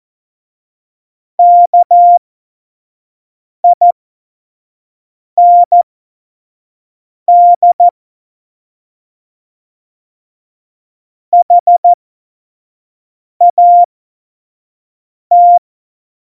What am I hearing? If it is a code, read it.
Morse code, heard as a keyed tone, about 14 wpm, spaced as for 5 wpm, 700 Hz. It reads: KIND HAT
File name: Morsekurs / Witze / Witz 16. Januar 2018, 25 F70